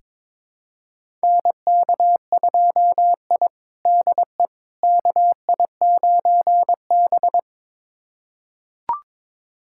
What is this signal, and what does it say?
Morse code, heard as a keyed tone, about 22 wpm, 700 Hz. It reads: NK2I DE KI9B
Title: USA Callsign DE Callsign - 4 Characters - 22wpm